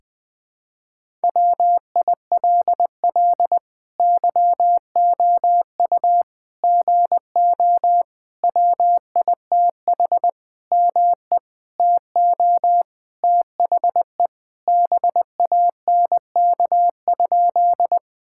Morse code, heard as a keyed tone, about 20 wpm, 700 Hz. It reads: WILL YOU GO WITH ME TO THE BANK?
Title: Sentences from Top 1000 Words - Review Part 6 - 20wpm